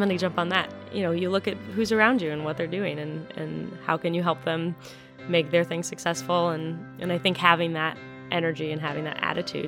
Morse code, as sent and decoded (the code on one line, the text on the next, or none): none